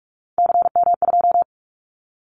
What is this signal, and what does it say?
Morse code, heard as a keyed tone, about 44 wpm, 700 Hz. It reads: CM2
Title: Callsign Prefixes for Top 100 DXCC - 45wpm